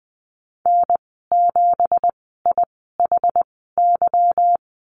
Morse code, heard as a keyed tone, about 19 wpm, spaced as for 14 wpm, 700 Hz. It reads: N7IHY
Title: 173 Lesson 28g - Number 9 - Call Signs 20wpm